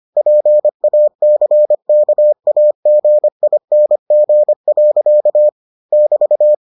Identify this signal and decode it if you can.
Morse code, heard as a keyed tone, about 25 wpm, 600 Hz. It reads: PACKAGING. <BT>